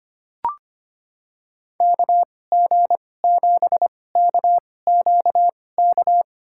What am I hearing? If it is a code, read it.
Morse code, heard as a keyed tone, about 25 wpm, spaced as for 18 wpm, 700 Hz. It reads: KG7KQK
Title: USA Call Signs - 6 Characters - 2x Character Spacing - 25wpm